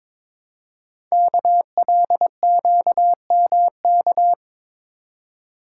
Morse code, heard as a keyed tone, about 22 wpm, 700 Hz. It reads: KLQMK
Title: Code Groups - 5 Letters - 22wpm